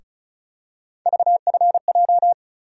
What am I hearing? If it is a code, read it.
Morse code, heard as a keyed tone, about 34 wpm, 700 Hz. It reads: VFJ
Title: USA Call Sign Postfixes - 3 Letters 35wpm